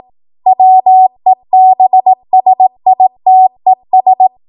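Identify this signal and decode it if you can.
Morse code, heard as a keyed tone, about 18 wpm, 750 Hz. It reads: WEBSITES
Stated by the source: ARRL 18 wpm Morse code practice archive transmission.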